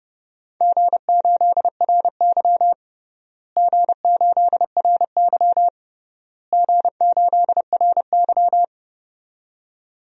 Morse code, heard as a keyed tone, about 30 wpm, 700 Hz. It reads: G8RY G8RY G8RY
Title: DXCC-100 Call Signs - 4 Characters - Repeated 3x - 30wpm